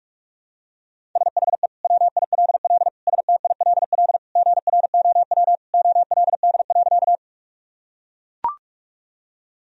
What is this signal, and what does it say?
Morse code, heard as a keyed tone, about 44 wpm, 700 Hz. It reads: SHE WILL STILL GROW OLD.
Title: Sentences from Top 1000 Words - Review Part 1 - 45wpm